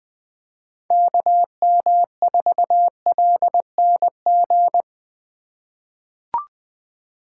Morse code, heard as a keyed tone, about 20 wpm, 700 Hz. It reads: KM4LNG
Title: USA Call Signs - 6 Characters - 3x Character Spacing - 20wpm